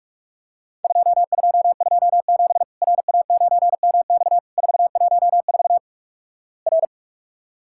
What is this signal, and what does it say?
Morse code, heard as a keyed tone, about 44 wpm, 700 Hz. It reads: 2227 RA9MX 414 R
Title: Worked All Europe DX Contest - QTC Exchange - 45wpm